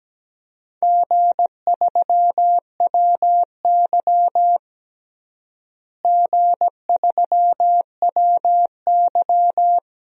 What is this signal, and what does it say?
Morse code, heard as a keyed tone, about 17 wpm, 700 Hz. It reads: G3WY G3WY
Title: DXCC-100 Call Signs - 4 Characters - Repeated 3x - 17wpm